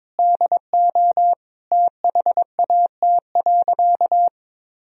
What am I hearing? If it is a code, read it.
Morse code, heard as a keyed tone, about 22 wpm, 700 Hz. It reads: DO THAT.
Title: Sentences from Top 1000 Words - Review Part 3 - 22wpm